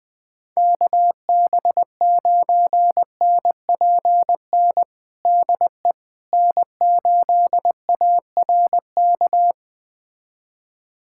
Morse code, heard as a keyed tone, about 21 wpm, 700 Hz. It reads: KB9NPN DE N8ARK
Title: USA Callsign DE Callsign 20wpm